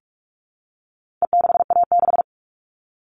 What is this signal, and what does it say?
Morse code, heard as a keyed tone, about 44 wpm, 700 Hz. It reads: E6A6
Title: All Cut Numbers - 4-Digit 45wpm